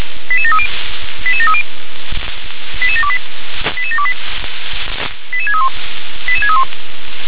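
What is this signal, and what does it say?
Five-tone selective call (ZVEI), heard as quick runs of five tones, tones of about 70 ms. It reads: 88730 88730 88728 88728 88631 88631
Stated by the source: ZVEI five-tone selective call received on SDR.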